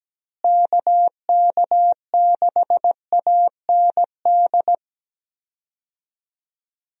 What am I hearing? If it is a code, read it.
Morse code, heard as a keyed tone, about 17 wpm, 700 Hz. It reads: KK6AND
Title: USA Call Signs - 6 Characters - Repeated 2x - 17wpm